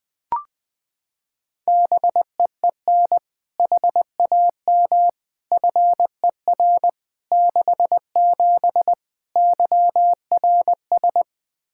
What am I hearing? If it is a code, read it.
Morse code, heard as a keyed tone, about 21 wpm, 700 Hz. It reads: BEEN HAM FER 67 YRS
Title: Practice QSO - Exchange 3 - More Info - 20wpm